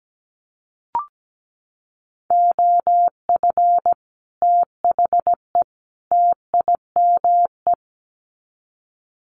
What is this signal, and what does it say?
Morse code, heard as a keyed tone, about 17 wpm, 700 Hz. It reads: OF THE TIME
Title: Sets of 3 Words 17wpm